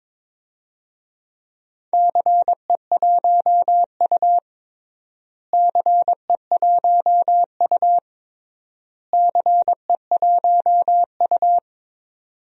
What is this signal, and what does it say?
Morse code, heard as a keyed tone, about 22 wpm, 700 Hz. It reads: CE1U CE1U CE1U
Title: DXCC-100 Call Signs - 4 Characters - Repeated 3x - 22wpm